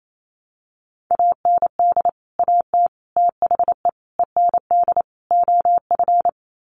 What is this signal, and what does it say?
Morse code, heard as a keyed tone, about 28 wpm, 700 Hz. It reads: AND AT THE END OF
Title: Sets of 5 Words 28wpm